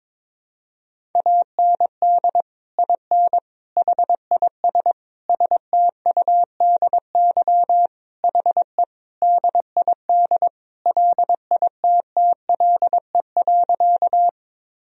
Morse code, heard as a keyed tone, about 22 wpm, 700 Hz. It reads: AND IN HIS STUDY HE DID LITTLE.
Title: Sentences from Top 1000 Words - Review Part 1 - 22wpm